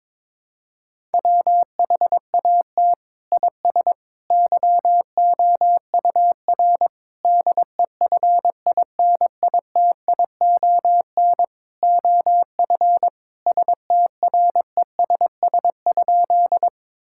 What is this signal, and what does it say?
Morse code, heard as a keyed tone, about 22 wpm, 700 Hz. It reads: WHAT IS YOUR DEFINITION OF STRESS?
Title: Sentences from Top 1000 Words - Review Part 6 - 22wpm